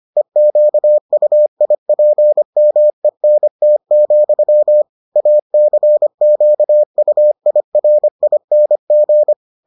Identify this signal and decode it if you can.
Morse code, heard as a keyed tone, about 25 wpm, 600 Hz. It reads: EQUIPMENT, ACQUIRING